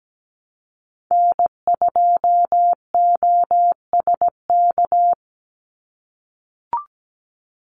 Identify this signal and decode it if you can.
Morse code, heard as a keyed tone, about 17 wpm, 700 Hz. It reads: N2OSK
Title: USA Call Signs 17wpm